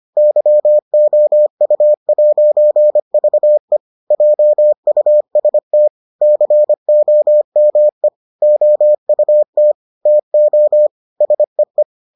Morse code, heard as a keyed tone, about 25 wpm, 600 Hz. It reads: YOU'VE JUST COME OUT TO SEE